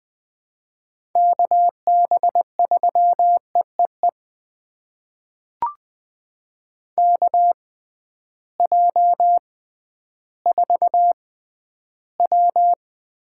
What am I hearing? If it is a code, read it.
Morse code, heard as a keyed tone, about 20 wpm, spaced as for 12 wpm, 700 Hz. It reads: KB3EEE KJ4W
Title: USA Call Signs - 6x Character Spacing - 20wpm